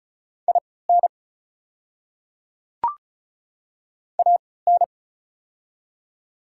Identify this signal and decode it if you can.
Morse code, heard as a keyed tone, about 36 wpm, spaced as for 20 wpm, 690 Hz. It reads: IN AN
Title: 3x Character Spacing - Top 100 Words - 35wpm